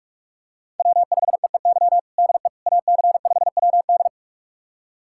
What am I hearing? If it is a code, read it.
Morse code, heard as a keyed tone, about 46 wpm, 700 Hz. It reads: W5EEY DE AC5WD
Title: USA Callsign DE Callsign - 5 Characters - 45wpm